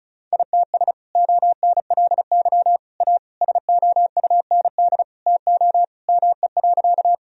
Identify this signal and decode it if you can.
Morse code, heard as a keyed tone, about 36 wpm, 700 Hz. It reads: ITS ONLY A SOUND TO ME.
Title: Sentences from Top 1000 Words - Review Part 1 - 35wpm